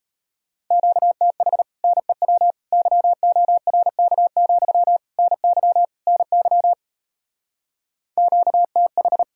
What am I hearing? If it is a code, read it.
Morse code, heard as a keyed tone, about 38 wpm, 700 Hz. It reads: QTH NEW YORK, NY NY QTH